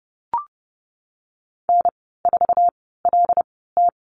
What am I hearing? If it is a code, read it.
Morse code, heard as a keyed tone, about 30 wpm, spaced as for 17 wpm, 700 Hz. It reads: N4LT
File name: Lesson 14g - Number 2 - Call Signs 30wpm